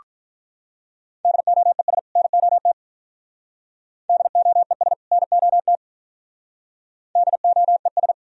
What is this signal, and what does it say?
Morse code, heard as a keyed tone, about 46 wpm, 700 Hz. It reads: DOES NOT DOES NOT DOES